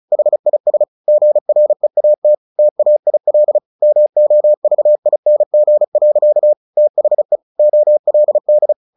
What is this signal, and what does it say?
Morse code, heard as a keyed tone, about 34 wpm, 600 Hz. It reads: HIS GREAT TAIL MOVING. THE OLD
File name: the old man and the sea 2 35wpm